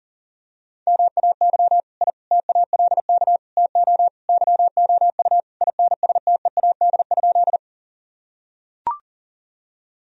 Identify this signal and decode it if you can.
Morse code, heard as a keyed tone, about 40 wpm, 700 Hz. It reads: MAY I TALK TO YOU INSTEAD?
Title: Sentences from Top 1000 Words - Review Part 4 - 40wpm